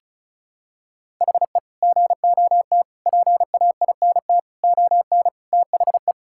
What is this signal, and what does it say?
Morse code, heard as a keyed tone, about 36 wpm, 700 Hz. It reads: HE GOT PAINT ON THE
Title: Sentences from Top 1000 Words - Review Part 6 - 35wpm